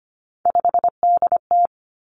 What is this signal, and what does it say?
Morse code, heard as a keyed tone, about 25 wpm, 700 Hz. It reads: 5DT
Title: Three Challenging Letter-Number Combinations 25wpm